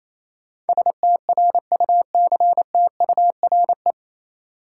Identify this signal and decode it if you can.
Morse code, heard as a keyed tone, about 28 wpm, 700 Hz. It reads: STRUCTURE
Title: Sentences from Top 1000 Words - Review Part 6 - New Word Drill - 28wpm